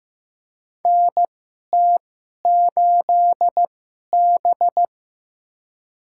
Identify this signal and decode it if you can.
Morse code, heard as a keyed tone, about 15 wpm, spaced as for 11 wpm, 700 Hz. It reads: NT8B
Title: USA Call Signs - 2x Character Spacing - 15wpm